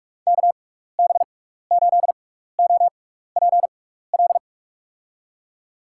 Morse code, heard as a keyed tone, about 45 wpm, spaced as for 15 wpm, 700 Hz. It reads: KB8OPL
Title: USA Call Signs - 6 Characters - 6x Character Spacing - 45wpm